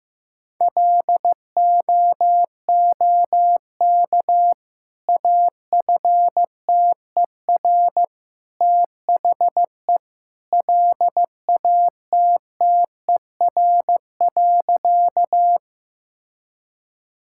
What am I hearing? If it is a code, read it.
Morse code, heard as a keyed tone, about 16 wpm, 700 Hz. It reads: LOOK AFTER THE LATTER.
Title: Sentences from Top 1600 Words 15wpm